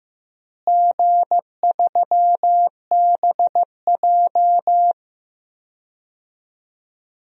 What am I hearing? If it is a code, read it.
Morse code, heard as a keyed tone, about 15 wpm, 700 Hz. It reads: G3BJ
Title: DXCC-100 Call Signs - 4 Characters - Repeated 2x - 15wpm